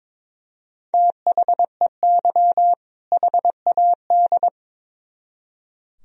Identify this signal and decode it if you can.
Morse code, heard as a keyed tone, about 26 wpm, 700 Hz. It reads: THEY HAD
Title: Speed-Racing - Sets of 2 Words 22wpm